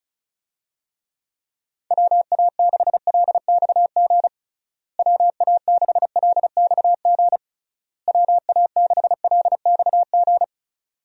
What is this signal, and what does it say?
Morse code, heard as a keyed tone, about 36 wpm, 700 Hz. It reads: WA6LXG WA6LXG WA6LXG
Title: USA Call Signs - 6 Characters - Repeated 3x - 35wpm